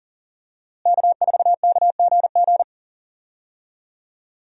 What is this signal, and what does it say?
Morse code, heard as a keyed tone, about 40 wpm, 700 Hz. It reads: K4KGG
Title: USA Call Signs - 5 Characters - Repeated 2x - 40wpm